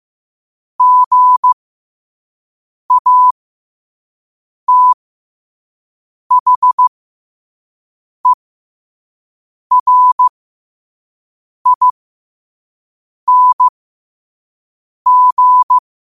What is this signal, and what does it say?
Morse code, heard as a keyed tone, about 15 wpm, spaced as for 5 wpm, 1000 Hz. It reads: GATHERING
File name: qotdfiles / 5wpm